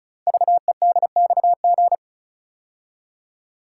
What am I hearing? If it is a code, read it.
Morse code, heard as a keyed tone, about 36 wpm, 700 Hz. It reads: VEDXG
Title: Code Groups - 5 Letters - 35wpm